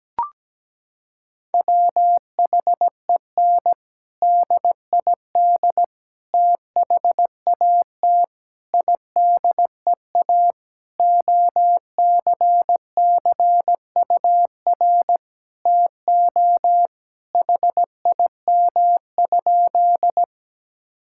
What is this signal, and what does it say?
Morse code, heard as a keyed tone, about 17 wpm, 700 Hz. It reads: WHEN DID THAT IDEA OCCUR TO HIM?
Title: Sentences from Top 1000 Words - Review Part 6 - 17wpm